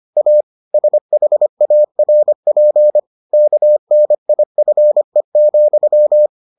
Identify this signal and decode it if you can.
Morse code, heard as a keyed tone, about 25 wpm, 600 Hz. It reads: A SHARP KNIFE,